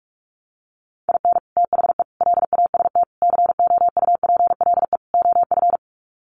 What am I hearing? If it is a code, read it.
Morse code, heard as a keyed tone, about 44 wpm, 700 Hz. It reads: IN THE LAST COUPLE OF
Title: Sets of 5 Words 45wpm